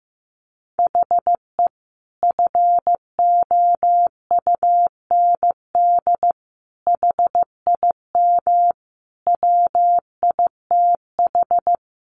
Morse code, heard as a keyed tone, about 15 wpm, 700 Hz. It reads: HE FOUND HIM WITH
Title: Sentences from Top 1500 Words 15wpm